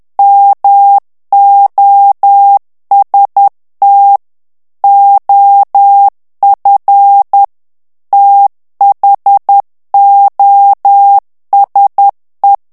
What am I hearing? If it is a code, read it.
Morse code, heard as a keyed tone, about 10 wpm, 800 Hz. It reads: MOST OF THOSE
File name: morse mp3 / ANGLAIS texte clair 2 vitesse 10 mots minute